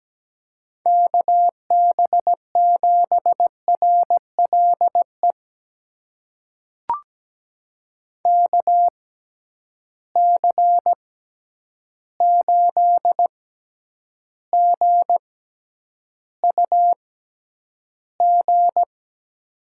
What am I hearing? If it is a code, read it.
Morse code, heard as a keyed tone, about 17 wpm, spaced as for 9 wpm, 700 Hz. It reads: KB7RLE KC8GUG
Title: USA Call Signs - 6 Characters - 6x Character Spacing - 17wpm